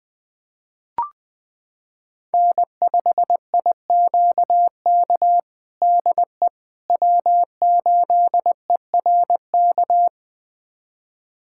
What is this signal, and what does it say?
Morse code, heard as a keyed tone, about 21 wpm, 700 Hz. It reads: N5IQK DE W8ERK
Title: USA Callsign DE Callsign - 5 Characters - 20wpm